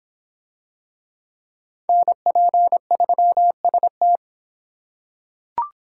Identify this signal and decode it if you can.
Morse code, heard as a keyed tone, about 24 wpm, 700 Hz. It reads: NP3ST